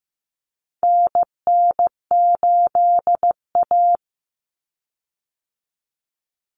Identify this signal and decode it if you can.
Morse code, heard as a keyed tone, about 15 wpm, 700 Hz. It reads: NN8A
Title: Common Cut Numbers - 4-Digit 15wpm